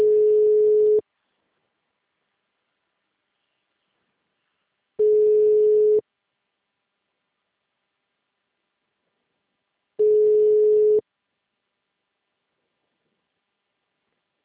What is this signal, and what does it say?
Ringback tone (Europe, 425 Hz).
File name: out- -2017- - - .1